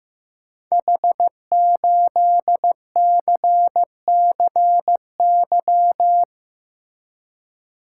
Morse code, heard as a keyed tone, about 15 wpm, 700 Hz. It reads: H8CCY
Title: Code Groups - 5 Letter:Numbers - 15wpm